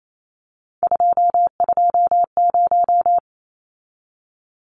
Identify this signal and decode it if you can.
Morse code, heard as a keyed tone, about 28 wpm, 700 Hz. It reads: 220
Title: Numbers - 3-Digit 28wpm